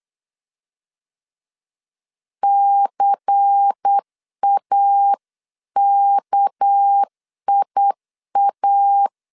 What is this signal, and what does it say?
Morse code, heard as a keyed tone, about 8 wpm, 800 Hz. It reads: CAKIA
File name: CW / DET / 1 6 1500C S B 8wpm 5142